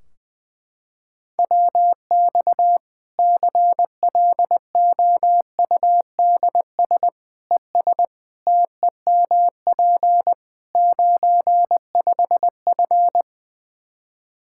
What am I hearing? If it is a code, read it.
Morse code, heard as a keyed tone, about 21 wpm, 700 Hz. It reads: WX CLOUDS ES TEMP 95F
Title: Practice QSO - Exchange 2 - Info - 20wpm